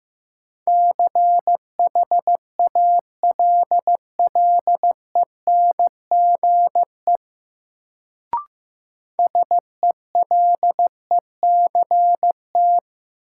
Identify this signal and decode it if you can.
Morse code, heard as a keyed tone, about 15 wpm, 700 Hz. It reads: CHALLENGE SELECT
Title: Sentences from Top 1300 Words - New Word Drill Alt - 15wpm